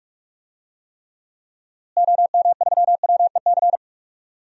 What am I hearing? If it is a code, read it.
Morse code, heard as a keyed tone, about 46 wpm, 700 Hz. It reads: OM3WEC
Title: DXCC-100 Call Signs - 6 Characters - Repeated 2x - 45wpm